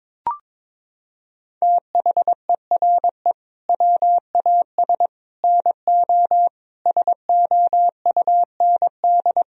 Morse code, heard as a keyed tone, about 21 wpm, 700 Hz. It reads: THERE WAS NO SOUND
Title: Sentences from Top 1000 Words - Review Part 3 - 22wpm